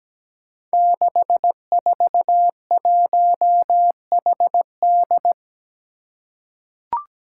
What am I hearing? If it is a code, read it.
Morse code, heard as a keyed tone, about 17 wpm, 700 Hz. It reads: 641HD
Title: Five Challenging Letter-Number Combinations 17wpm